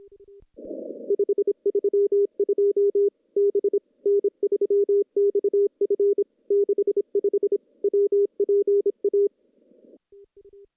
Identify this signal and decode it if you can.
Morse code, heard as a keyed tone, about 27 wpm, 410 Hz. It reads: 532 B N3XF 65 WPA
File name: contests / CW-SS-k5tr- -0400